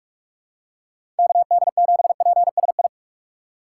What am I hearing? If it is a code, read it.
Morse code, heard as a keyed tone, about 46 wpm, 700 Hz. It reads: KD7PSI
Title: USA Call Signs - 6 Characters - 2x Character Spacing - 45wpm